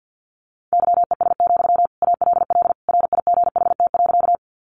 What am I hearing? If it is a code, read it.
Morse code, heard as a keyed tone, about 50 wpm, 700 Hz. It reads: YES, ALL RIGHT.
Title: Sentences from Top 500 Words 50wpm